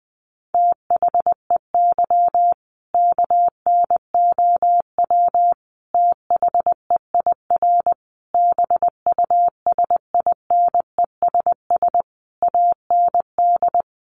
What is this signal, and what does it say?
Morse code, heard as a keyed tone, about 20 wpm, 700 Hz. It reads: THEY KNOW THEIR BUSINESS AND
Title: Sentences from Top 1200 Words 20wpm